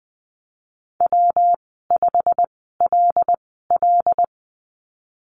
Morse code, heard as a keyed tone, about 19 wpm, spaced as for 14 wpm, 700 Hz. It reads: W5LL
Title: 161 Lesson 26g - Letter g - Call Signs 20wpm